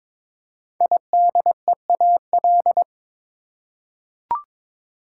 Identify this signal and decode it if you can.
Morse code, heard as a keyed tone, about 22 wpm, 700 Hz. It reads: IDEAL
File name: 5 Letter Words - First 3 Letters Given - 22wpm